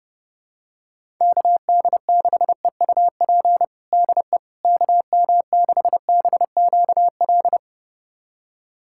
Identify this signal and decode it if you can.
Morse code, heard as a keyed tone, about 30 wpm, 700 Hz. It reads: KD6EUP DE KM6BQL
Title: USA Callsign DE Callsign - 6 Characters - 30wpm